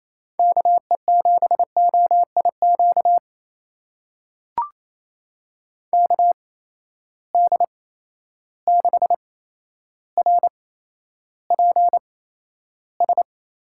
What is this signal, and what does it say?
Morse code, heard as a keyed tone, about 28 wpm, spaced as for 12 wpm, 700 Hz. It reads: KE7OIQ KD6RPS
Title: USA Call Signs - 6 Characters - 8x Character Spacing - 28wpm